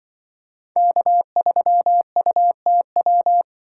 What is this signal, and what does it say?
Morse code, heard as a keyed tone, about 24 wpm, 700 Hz. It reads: K3UTW
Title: Speed Racing - USA 1x3 Call Signs 20wpm